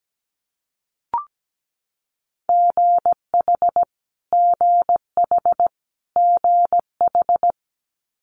Morse code, heard as a keyed tone, about 17 wpm, 700 Hz. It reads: GH GH GH
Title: Multi-letter Phonemes Repeated 3x 17wpm